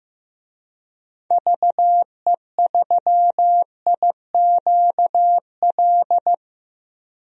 Morse code, heard as a keyed tone, about 15 wpm, 700 Hz. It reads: VE3IQL
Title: DXCC-100 Call Signs - 6 Characters - 15wpm